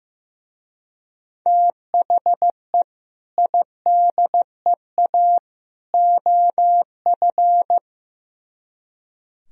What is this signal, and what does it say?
Morse code, heard as a keyed tone, about 16 wpm, 700 Hz. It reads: THE IDEA OF
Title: Speed-Racing - Sets of 3 Words 15wpm